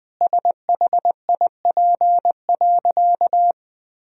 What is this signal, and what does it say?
Morse code, heard as a keyed tone, about 20 wpm, 700 Hz. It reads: SHIP.
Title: Sentences from Top 1000 Words - Review Part 2 - 20wpm